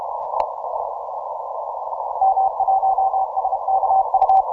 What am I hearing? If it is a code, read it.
Morse code, heard as a keyed tone, about 31 wpm, 770 Hz, with noise. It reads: M1SKU
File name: simple-file-list / TO5M 17MCW